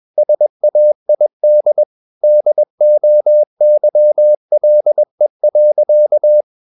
Morse code, heard as a keyed tone, about 21 wpm, 600 Hz. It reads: SAID DOYLE.
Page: CW med Gnister